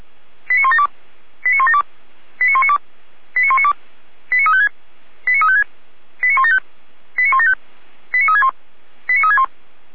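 Five-tone selective call (ZVEI), heard as quick runs of five tones, tones of about 70 ms. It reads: 89172 89172 89192 89192 89356 89356 89175 89175 89361 89361
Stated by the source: Received on an SDR, ZVEI five-tone call 89108, from the feuerwehr-orlamuende.org audio archive (Feuerwehr Orlamünde).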